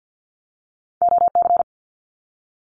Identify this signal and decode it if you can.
Morse code, heard as a keyed tone, about 50 wpm, 700 Hz. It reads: OC
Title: Abbreviations Repeated 3x - 50wpm